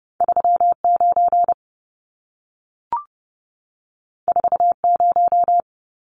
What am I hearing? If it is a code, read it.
Morse code, heard as a keyed tone, about 30 wpm, 700 Hz. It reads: 39 40
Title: Numbers - 2-Digit 30wpm